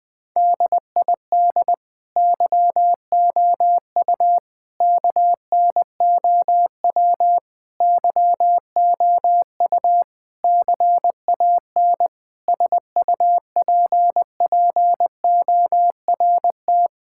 Morse code, heard as a keyed tone, about 20 wpm, 700 Hz. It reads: DID YOU KNOW YOU CAN SUPPORT
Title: Sets of 9 Words - 20wpm